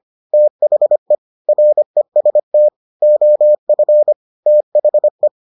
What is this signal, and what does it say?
Morse code, heard as a keyed tone, about 25 wpm, 600 Hz. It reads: THE REST OF THE